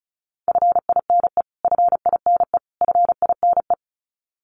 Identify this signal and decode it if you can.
Morse code, heard as a keyed tone, about 34 wpm, 700 Hz. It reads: FINE FINE FINE
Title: Top 500 Words Repeated 35wpm